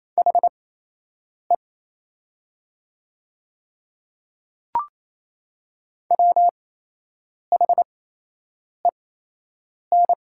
Morse code, heard as a keyed tone, about 28 wpm, spaced as for 8 wpm, 700 Hz. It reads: HE WHEN
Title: 8x Character Spacing - Top 100 Words - 28wpm